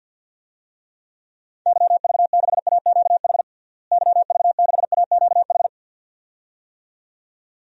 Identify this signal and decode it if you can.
Morse code, heard as a keyed tone, about 52 wpm, 700 Hz. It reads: YV6AQH YV6AQH
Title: DXCC-100 Call Signs - 6 Characters - Repeated 2x - 50wpm